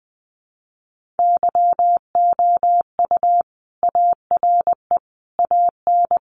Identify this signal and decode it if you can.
Morse code, heard as a keyed tone, about 20 wpm, 700 Hz. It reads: YOU ARE AN
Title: Sentences from Top 1400 Words 20wpm